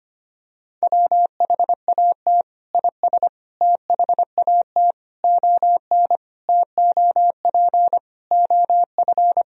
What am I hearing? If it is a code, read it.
Morse code, heard as a keyed tone, about 25 wpm, 700 Hz. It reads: WHAT IS THAT ON TOP OF
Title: Sentences from Top 1000 Words - Review Part 2 - 25wpm